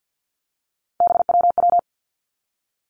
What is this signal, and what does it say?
Morse code, heard as a keyed tone, about 50 wpm, 700 Hz. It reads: BWW
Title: Three Challenging Letter-Number Combinations 50wpm